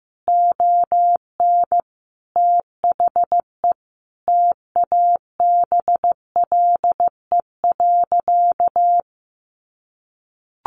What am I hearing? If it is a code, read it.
Morse code, heard as a keyed tone, about 14 wpm, 700 Hz. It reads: ON THE TABLE.
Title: Sentences from Top 1200 Words 15wpm